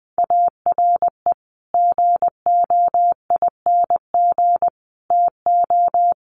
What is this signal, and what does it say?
Morse code, heard as a keyed tone, about 20 wpm, 700 Hz. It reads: ARE GOING TO
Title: Sets of 4 Words 20wpm